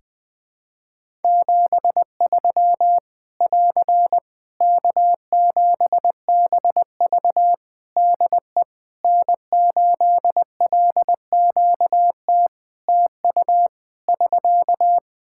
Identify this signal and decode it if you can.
Morse code, heard as a keyed tone, about 20 wpm, 700 Hz. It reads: 73 <AR> K7BV DE N8LQT TU <SK>
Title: Practice QSO - Exchange 4 - Ending - 20wpm